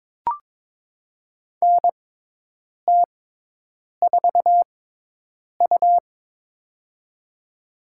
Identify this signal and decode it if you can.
Morse code, heard as a keyed tone, about 22 wpm, spaced as for 8 wpm, 700 Hz. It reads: NT4U
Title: USA Call Signs - 4 Characters - 6x Character Spacing - 22wpm